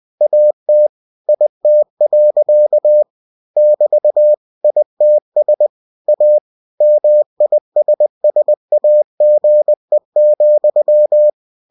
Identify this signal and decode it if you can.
Morse code, heard as a keyed tone, about 20 wpm, 600 Hz. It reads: AT IT. <BT> ITS A MISSAGE,